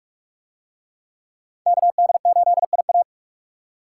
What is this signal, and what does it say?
Morse code, heard as a keyed tone, about 38 wpm, 700 Hz. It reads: KD8IA